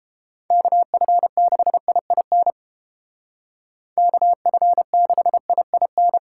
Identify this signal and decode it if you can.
Morse code, heard as a keyed tone, about 29 wpm, 700 Hz. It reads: KF6IIN KF6IIN